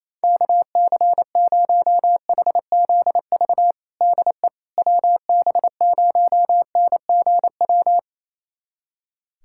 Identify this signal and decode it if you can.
Morse code, heard as a keyed tone, about 28 wpm, 700 Hz. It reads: KC0HZV DE WB0NGW
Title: USA Callsign DE Callsign - 6 Characters - 28wpm